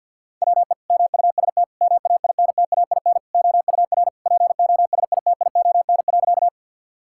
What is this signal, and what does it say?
Morse code, heard as a keyed tone, about 50 wpm, 700 Hz. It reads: WE MUST MAINTAIN OUR POSITION.
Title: Sentences from Top 1000 Words - Review Part 5 - 50wpm